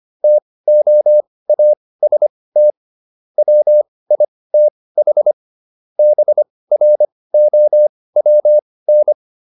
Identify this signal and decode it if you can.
Morse code, heard as a keyed tone, about 25 wpm, spaced as for 18 wpm, 600 Hz. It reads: TOAST WITH BROWN